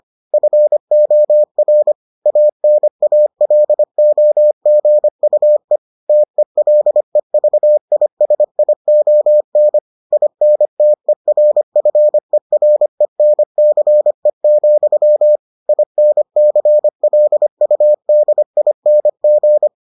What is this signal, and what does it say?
Morse code, heard as a keyed tone, about 25 wpm, 600 Hz. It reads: FOR ANALOGUE TELEVISION INTERFERENCE, INCLUDING